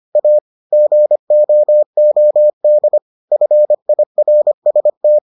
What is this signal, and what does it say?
Morse code, heard as a keyed tone, about 25 wpm, 600 Hz. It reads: A GOOD FIRST